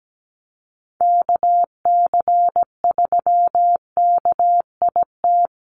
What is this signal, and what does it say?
Morse code, heard as a keyed tone, about 17 wpm, 700 Hz. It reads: KC3KIT
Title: USA Call Signs Repeated 3x 17wpm